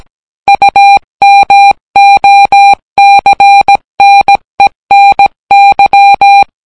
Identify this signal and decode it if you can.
Morse code, heard as a keyed tone, about 17 wpm, 800 Hz. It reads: UMOCNENY